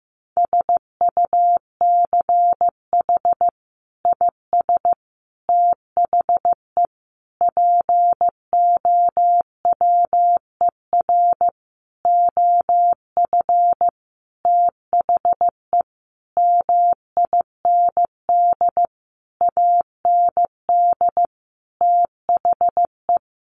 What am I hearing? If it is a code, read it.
Morse code, heard as a keyed tone, about 15 wpm, 700 Hz. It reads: SUCH IS THE POWER OF THE MIND AND THE
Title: Sentences from Top 1400 Words 15wpm